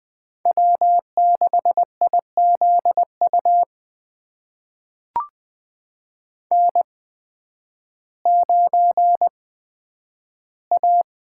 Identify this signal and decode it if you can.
Morse code, heard as a keyed tone, about 20 wpm, spaced as for 11 wpm, 700 Hz. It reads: W6IZU N9A
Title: USA Call Signs - 5 Characters - 8x Character Spacing - 20wpm